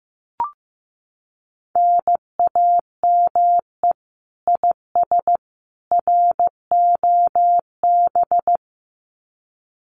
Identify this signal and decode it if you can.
Morse code, heard as a keyed tone, about 15 wpm, 700 Hz. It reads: NAME IS ROB
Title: Common QSO Phrases 15wpm